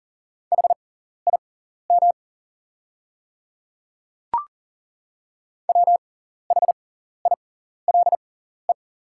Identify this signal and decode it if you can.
Morse code, heard as a keyed tone, about 41 wpm, spaced as for 14 wpm, 690 Hz. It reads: HIM WHILE
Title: 6x Character Spacing - Top 300 Words - 40wpm